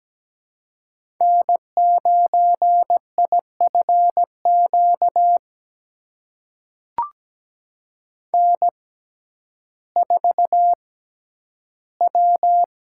Morse code, heard as a keyed tone, about 17 wpm, spaced as for 10 wpm, 700 Hz. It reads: N9IFQ N4W
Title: USA Call Signs - 5 Characters - 6x Character Spacing - 17wpm